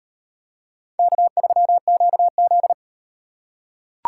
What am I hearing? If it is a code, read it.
Morse code, heard as a keyed tone, about 38 wpm, 700 Hz. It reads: K3QZ